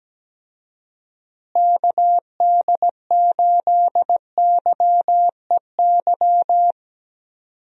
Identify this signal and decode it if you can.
Morse code, heard as a keyed tone, about 17 wpm, 700 Hz. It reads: KD8YEY
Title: USA Call Signs - 6 Characters - Repeated 2x - 17wpm